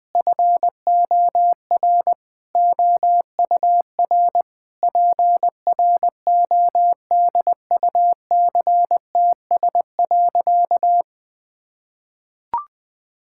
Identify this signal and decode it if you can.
Morse code, heard as a keyed tone, about 19 wpm, 700 Hz. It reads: FOR OUR PRODUCTS.
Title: Sentences with 6-Word Phrases - 20wpm